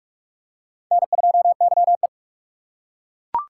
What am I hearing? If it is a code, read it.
Morse code, heard as a keyed tone, about 41 wpm, 700 Hz. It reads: N2YE